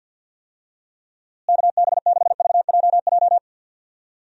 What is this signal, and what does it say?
Morse code, heard as a keyed tone, about 50 wpm, 700 Hz. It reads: KB6VJJ
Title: USA Call Signs - 6 Characters - Repeated 3x - 50wpm